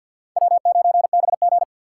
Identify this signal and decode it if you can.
Morse code, heard as a keyed tone, about 49 wpm, 700 Hz. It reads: W9BG
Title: USA Call Signs - 4x Character Spacing - 50wpm